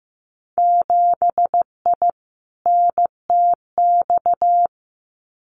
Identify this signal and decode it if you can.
Morse code, heard as a keyed tone, about 15 wpm, 700 Hz. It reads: 7I NTX
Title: Winter Field Day Exchanges Encore 15wpm